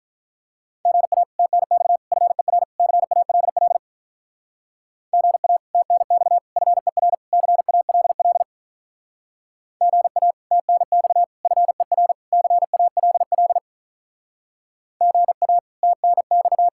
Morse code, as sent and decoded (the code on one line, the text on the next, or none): --. .- / - -. -..- / ..-. .-. / -.-. .- .-.. .-.. / --. .- / - -. -..- / ..-. .-. / -.-. .- .-.. .-.. / --. .- / - -. -..- / ..-. .-. / -.-. .- .-.. .-.. / --. .- / - -. -..-
GA TNX FER CALL GA TNX FER CALL GA TNX FER CALL GA TNX